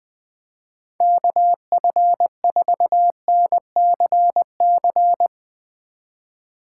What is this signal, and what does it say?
Morse code, heard as a keyed tone, about 21 wpm, 700 Hz. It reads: KF4NCC